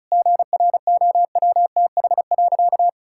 Morse code, heard as a keyed tone, about 36 wpm, 700 Hz. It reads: GROWTH.
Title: Sentences from Top 1000 Words - Review Part 5 - 35wpm